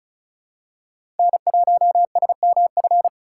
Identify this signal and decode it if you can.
Morse code, heard as a keyed tone, about 34 wpm, 700 Hz. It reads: N1SMF
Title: USA Call Signs - 5 Characters - 4x Character Spacing - 35wpm